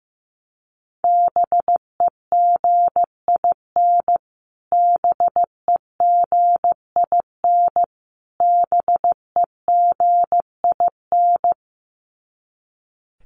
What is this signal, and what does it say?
Morse code, heard as a keyed tone, about 15 wpm, 700 Hz. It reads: BEGIN BEGIN BEGIN
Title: Top 500 Words Repeated 15wpm